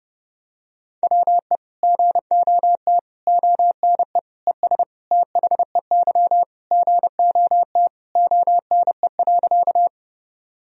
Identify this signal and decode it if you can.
Morse code, heard as a keyed tone, about 30 wpm, 700 Hz. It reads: WE GOT ONE ES THEY GOT ONE.
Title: Sentences from Top 300 Words and Abbreviations 30wpm